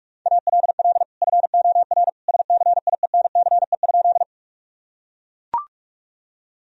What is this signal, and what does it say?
Morse code, heard as a keyed tone, about 43 wpm, 700 Hz. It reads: ALL FOR SCIENCE?
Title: Sentences from Top 1000 Words - Review Part 2 - 45wpm